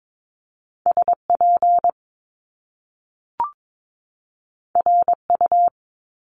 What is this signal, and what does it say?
Morse code, heard as a keyed tone, about 22 wpm, 700 Hz. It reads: SP RU
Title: Top Two Letter Combinations in Words 22wpm